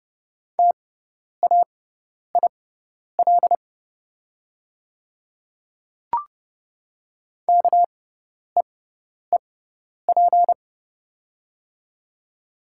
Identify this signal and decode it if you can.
Morse code, heard as a keyed tone, about 30 wpm, spaced as for 10 wpm, 700 Hz. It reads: TAIL KEEP
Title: 6x Character Spacing - Top 500 Words - 30wpm